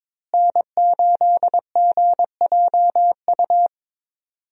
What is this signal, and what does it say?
Morse code, heard as a keyed tone, about 22 wpm, 700 Hz. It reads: N8GJU
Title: USA Call Signs - 5 Characters - Repeated 2x - 22wpm